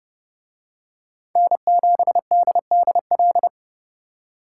Morse code, heard as a keyed tone, about 29 wpm, 700 Hz. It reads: N7DDL